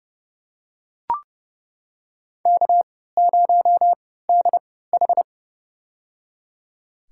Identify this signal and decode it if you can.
Morse code, heard as a keyed tone, about 30 wpm, spaced as for 17 wpm, 700 Hz. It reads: K0DH
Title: USA Call Signs - 3x Character Spacing - 30wpm